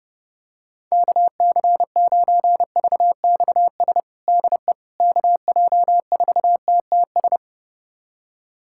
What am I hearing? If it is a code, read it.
Morse code, heard as a keyed tone, about 30 wpm, 700 Hz. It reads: KC9VXS DE KJ4TTS
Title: USA Callsign DE Callsign - 6 Characters - 30wpm